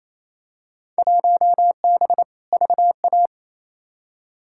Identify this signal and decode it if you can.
Morse code, heard as a keyed tone, about 28 wpm, 700 Hz. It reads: 1B VA
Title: Field Day Exchanges Encore - 28wpm